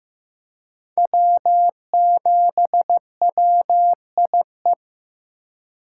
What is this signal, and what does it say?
Morse code, heard as a keyed tone, about 15 wpm, 700 Hz. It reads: W7WIE
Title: USA Call Signs - 2x Character Spacing - 15wpm